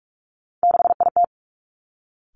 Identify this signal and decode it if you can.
Morse code, heard as a keyed tone, about 46 wpm, 700 Hz. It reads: 6IT
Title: Three Challenging Letter-Number Combinations 45wpm